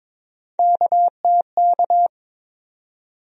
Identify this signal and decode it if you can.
Morse code, heard as a keyed tone, about 22 wpm, 700 Hz. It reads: KTK
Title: USA Call Sign Postfixes - 3 Letters 22wpm